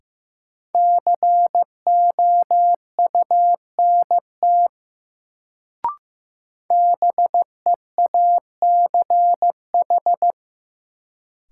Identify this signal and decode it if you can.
Morse code, heard as a keyed tone, about 15 wpm, 700 Hz. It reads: COUNT BEACH
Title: Sentences from Top 1500 Words - New Word Drill - 15wpm